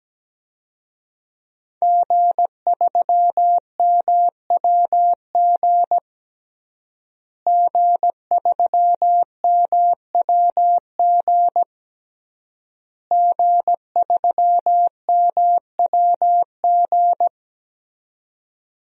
Morse code, heard as a keyed tone, about 17 wpm, 700 Hz. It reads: G3MWG G3MWG G3MWG
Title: DXCC-100 Call Signs - 5 Characters - Repeated 3x - 17wpm